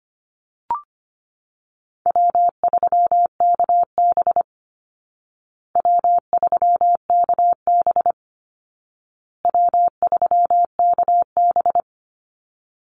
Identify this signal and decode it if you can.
Morse code, heard as a keyed tone, about 25 wpm, 700 Hz. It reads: W3KB W3KB W3KB
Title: Easy Call Signs Repeated 3x - 25wpm